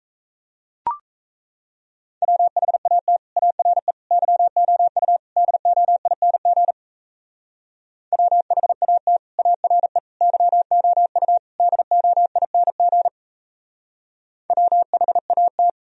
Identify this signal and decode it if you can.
Morse code, heard as a keyed tone, about 38 wpm, 700 Hz. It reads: WHAT ARE YOU DOING WHAT ARE YOU DOING WHAT